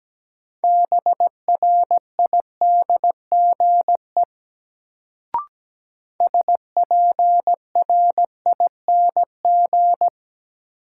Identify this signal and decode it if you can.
Morse code, heard as a keyed tone, about 17 wpm, 700 Hz. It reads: BRIDGE SPRING
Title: Sentences from Top 1200 Words - New Word Drill - 17wpm